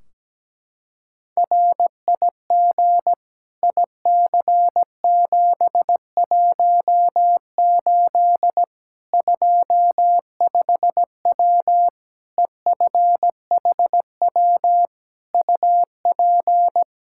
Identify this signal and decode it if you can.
Morse code, heard as a keyed tone, about 17 wpm, 700 Hz. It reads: RIG IC718 25W EFHW UP
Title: Practice QSO - Exchange 2 - Info - 17wpm